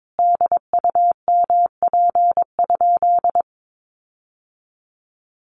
Morse code, heard as a keyed tone, about 22 wpm, 700 Hz. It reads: DUMP?
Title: Sentences from Top 1300 Words 22wpm